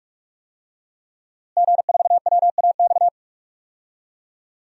Morse code, heard as a keyed tone, about 45 wpm, 700 Hz. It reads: G4WAX
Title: DXCC-100 Call Signs - 5 Characters - Repeated 3x - 45wpm